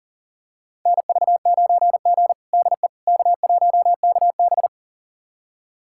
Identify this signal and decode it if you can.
Morse code, heard as a keyed tone, about 40 wpm, 700 Hz. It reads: NV9G DE K1KB
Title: USA Callsign DE Callsign - 4 Characters - 40wpm